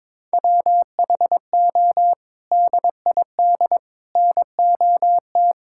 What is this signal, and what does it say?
Morse code, heard as a keyed tone, about 21 wpm, 700 Hz. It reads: WHO DID NOT
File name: practice / 22wpm / Sets of 6 Words - 22wpm